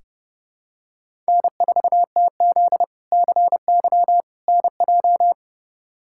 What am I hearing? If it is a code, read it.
Morse code, heard as a keyed tone, about 30 wpm, 700 Hz. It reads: N4TZ CY NJ
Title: K1USN SST QSO Practice - 30wpm